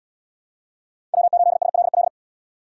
Morse code, heard as a keyed tone, about 77 wpm, 700 Hz. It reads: W8SLL